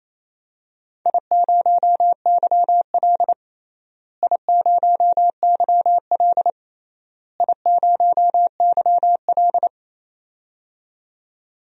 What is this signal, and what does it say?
Morse code, heard as a keyed tone, about 28 wpm, 700 Hz. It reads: I0YL I0YL I0YL
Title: DXCC-100 Call Signs - 4 Characters - Repeated 3x - 28wpm